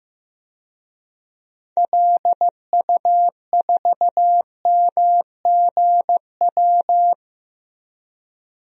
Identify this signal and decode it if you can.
Morse code, heard as a keyed tone, about 15 wpm, 700 Hz. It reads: LU4MGW
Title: DXCC-100 Call Signs - 6 Characters - 15wpm